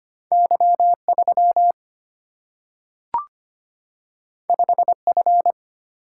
Morse code, heard as a keyed tone, about 25 wpm, 700 Hz. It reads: Y 3 5 F
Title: Single-Letter Single-Number 25wpm